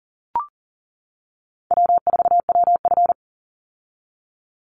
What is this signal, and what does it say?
Morse code, heard as a keyed tone, about 40 wpm, 700 Hz. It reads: W4WF
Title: Call Signs - 40wpm